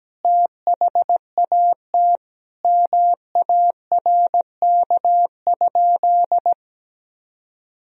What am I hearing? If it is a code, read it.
Morse code, heard as a keyed tone, about 17 wpm, 700 Hz. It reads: THAT MARK?
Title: Sentences from Top 1000 Words - Review Part 1 - 17wpm